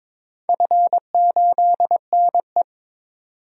Morse code, heard as a keyed tone, about 22 wpm, 700 Hz. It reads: F8NE
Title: DXCC-100 Call Signs - 4 Characters - 22wpm